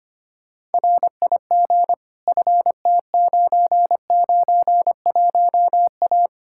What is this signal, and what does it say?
Morse code, heard as a keyed tone, about 33 wpm, 700 Hz. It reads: RIG FT991A